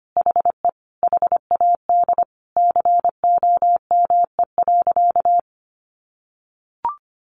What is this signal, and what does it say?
Morse code, heard as a keyed tone, about 25 wpm, 700 Hz. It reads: HE HAD COME.
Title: Sentences from Top 1500 Words 25wpm